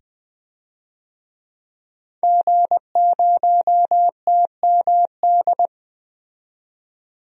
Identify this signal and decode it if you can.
Morse code, heard as a keyed tone, about 20 wpm, 700 Hz. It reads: G0TMD
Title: DXCC-100 Call Signs - 5 Characters - 20wpm